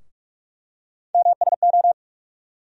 Morse code, heard as a keyed tone, about 46 wpm, 700 Hz. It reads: MSO
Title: USA Call Sign Postfixes - 3 Letters 45wpm